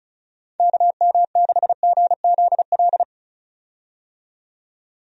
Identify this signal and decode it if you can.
Morse code, heard as a keyed tone, about 36 wpm, 700 Hz. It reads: KM6GZL
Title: USA Call Signs - 6 Characters - 35wpm